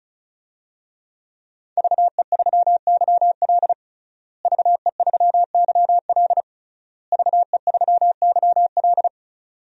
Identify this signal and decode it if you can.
Morse code, heard as a keyed tone, about 34 wpm, 700 Hz. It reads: VE3YL VE3YL VE3YL
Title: DXCC-100 Call Signs - 5 Characters - Repeated 3x - 35wpm